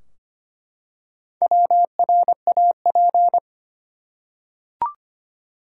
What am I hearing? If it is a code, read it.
Morse code, heard as a keyed tone, about 25 wpm, 700 Hz. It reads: WRAP
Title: Four Letter Words - 25wpm